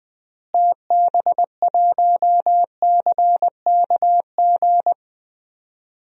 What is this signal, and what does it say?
Morse code, heard as a keyed tone, about 21 wpm, 700 Hz. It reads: TB1CKG
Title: DXCC-100 Call Signs - 6 Characters - Repeated 3x - 20wpm